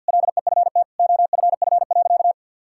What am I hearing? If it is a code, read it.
Morse code, heard as a keyed tone, about 52 wpm, 700 Hz. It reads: LEFT OFF.
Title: Sentences from Top 1000 Words - Review Part 6 - 50wpm